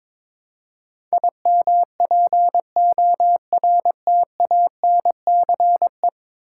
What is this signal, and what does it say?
Morse code, heard as a keyed tone, about 22 wpm, 700 Hz. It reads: IMPORTANCE
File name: practice / 22wpm / 10 Letter Words - First 3 Letters Given - 22wpm